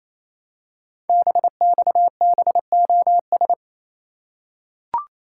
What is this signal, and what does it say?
Morse code, heard as a keyed tone, about 27 wpm, 700 Hz. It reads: BXBOS
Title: Code Groups - 5 Letters - 28wpm